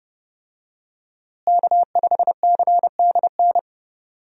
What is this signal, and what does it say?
Morse code, heard as a keyed tone, about 30 wpm, 700 Hz. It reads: K5CDN
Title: USA Call Signs - 5 Characters - Repeated 3x - 30wpm